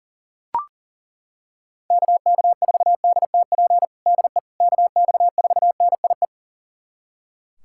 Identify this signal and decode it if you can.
Morse code, heard as a keyed tone, about 37 wpm, 700 Hz. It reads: KK4DTP DE KX4NIE
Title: USA Callsign DE Callsign - 6 Characters - 40wpm